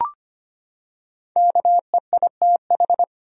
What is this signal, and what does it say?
Morse code, heard as a keyed tone, about 25 wpm, 700 Hz. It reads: KEITH
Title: USA Common Names - 25wpm